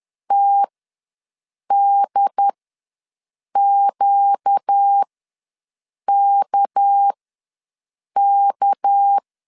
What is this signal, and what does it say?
Morse code, heard as a keyed tone, about 10 wpm, spaced as for 6 wpm, 800 Hz. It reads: TDQKK
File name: LEZIONI / 6 PKX0 / 6 PK ws 10wpm 3745